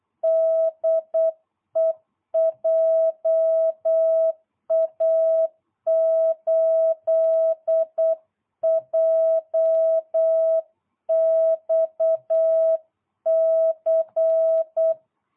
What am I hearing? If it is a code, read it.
Morse code, heard as a keyed tone, about 8 wpm, 660 Hz. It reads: DE JA8JXC